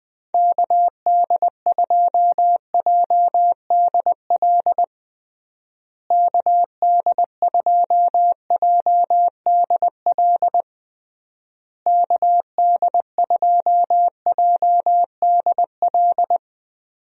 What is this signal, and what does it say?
Morse code, heard as a keyed tone, about 21 wpm, 700 Hz. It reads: KD2JDL KD2JDL KD2JDL
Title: USA Call Signs - 6 Characters - Repeated 3x - 20wpm